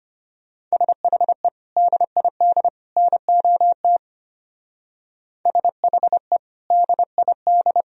Morse code, heard as a keyed tone, about 31 wpm, 700 Hz. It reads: SHE DID NOT SHE DID
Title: Speed-Racing - Sets of 3 Words 25wpm